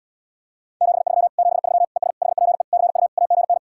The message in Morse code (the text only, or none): 73 73 ES HPE CU AGN